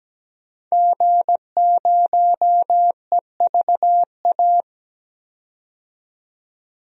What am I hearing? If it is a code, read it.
Morse code, heard as a keyed tone, about 17 wpm, 700 Hz. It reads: G0EVA
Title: DXCC-100 Call Signs - 5 Characters - 17wpm